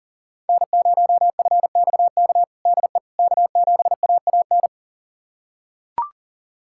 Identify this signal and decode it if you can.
Morse code, heard as a keyed tone, about 37 wpm, 700 Hz. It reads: N0FXK DE K7AAN
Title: USA Callsign DE Callsign - 5 Characters - 40wpm